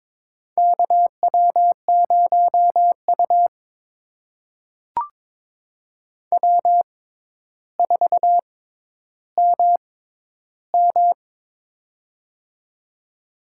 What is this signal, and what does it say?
Morse code, heard as a keyed tone, about 22 wpm, spaced as for 11 wpm, 700 Hz. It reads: KW0U W4MM
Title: USA Call Signs - 4 Characters - 6x Character Spacing - 22wpm